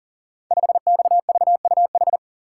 Morse code, heard as a keyed tone, about 41 wpm, 690 Hz. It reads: 5XVUH
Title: Code Groups - 5 Letter:Numbers - 40wpm